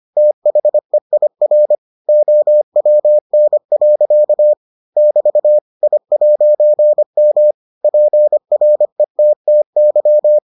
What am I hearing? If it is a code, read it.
Morse code, heard as a keyed tone, about 25 wpm, 600 Hz. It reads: THEIR OWN. <BT> I'M PRETTY